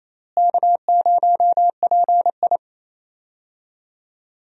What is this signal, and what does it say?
Morse code, heard as a keyed tone, about 28 wpm, 700 Hz. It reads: K0PI
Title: USA Call Signs - 4 Characters - Repeated 3x - 28wpm